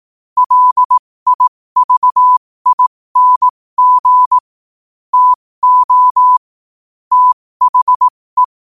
Morse code, heard as a keyed tone, about 18 wpm, spaced as for 15 wpm, 1000 Hz. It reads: LIVING TO THE
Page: New quotes every day in morse code at 15 Words per minute.